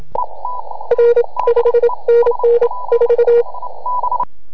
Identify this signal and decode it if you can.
Morse code, heard as a keyed tone, about 27 wpm, 470 Hz. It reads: R 5NN 4